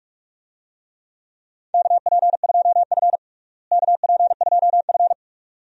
Morse code, heard as a keyed tone, about 45 wpm, 700 Hz. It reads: KP2F KP2F
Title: DXCC-100 Call Signs - 4 Characters - Repeated 2x - 45wpm